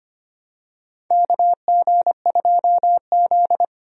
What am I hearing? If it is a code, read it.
Morse code, heard as a keyed tone, about 25 wpm, 700 Hz. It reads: KG2Z
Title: USA Call Signs - 4 Characters - 25wpm